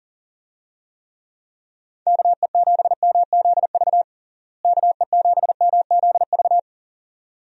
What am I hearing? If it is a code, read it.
Morse code, heard as a keyed tone, about 40 wpm, 700 Hz. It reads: KE7MZV KE7MZV
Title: USA Call Signs - 6 Characters - Repeated 2x - 40wpm